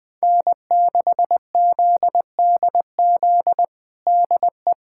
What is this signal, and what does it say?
Morse code, heard as a keyed tone, about 21 wpm, 700 Hz. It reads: N6ZDZ DE
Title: USA Callsign DE Callsign - 5 Characters - 20wpm